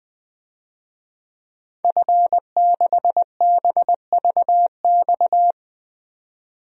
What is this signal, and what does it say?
Morse code, heard as a keyed tone, about 20 wpm, 700 Hz. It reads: F6BVX
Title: DXCC-100 Call Signs - 5 Characters - 20wpm